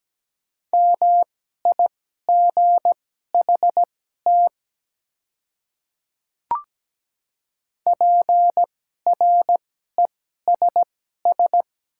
Morse code, heard as a keyed tone, about 17 wpm, spaced as for 12 wpm, 700 Hz. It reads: MIGHT PRESS
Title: 2x Character Spacing - Top 300 Words - 17wpm